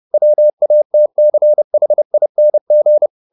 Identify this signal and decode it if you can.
Morse code, heard as a keyed tone, about 30 wpm, 600 Hz. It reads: WATCHING
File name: the old man and the sea / the old man and the sea 1 30wpm